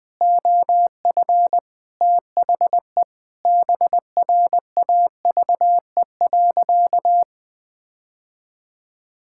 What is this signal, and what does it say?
Morse code, heard as a keyed tone, about 21 wpm, 700 Hz. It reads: OF THE BRAVE.
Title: Sentences with 9-Word Phrases - Spoken Beginning - 20wpm